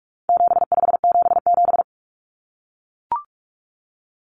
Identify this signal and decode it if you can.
Morse code, heard as a keyed tone, about 44 wpm, 700 Hz. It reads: 7577
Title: Common Cut Numbers - 4-Digit 45wpm